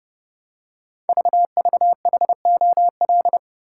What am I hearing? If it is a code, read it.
Morse code, heard as a keyed tone, about 30 wpm, 700 Hz. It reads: VVHOL
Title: Code Groups - 5 Letters - 30wpm